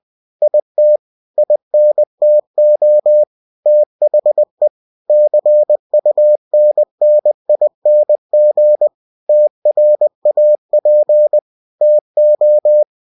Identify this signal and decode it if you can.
Morse code, heard as a keyed tone, about 20 wpm, 600 Hz. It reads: IT INTO THE CUNNING TRAP TO
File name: cw-books / winnie-the-pooh / winnie-20wpm-page-0039